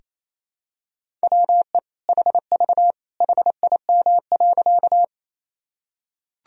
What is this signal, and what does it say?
Morse code, heard as a keyed tone, about 28 wpm, 700 Hz. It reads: WE HV HIM.
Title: Sentences from Top 100 Words and Abbreviations 28wpm